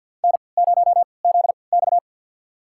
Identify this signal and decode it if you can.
Morse code, heard as a keyed tone, about 49 wpm, spaced as for 28 wpm, 700 Hz. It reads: N0ZX
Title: USA Call Signs - 4 Characters - 3x Character Spacing - 50wpm